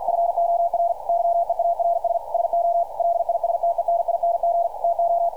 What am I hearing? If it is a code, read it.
Morse code, heard as a keyed tone, about 52 wpm, 680 Hz. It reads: FOR YEARS SO LISTENING TO
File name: archived / CRSnet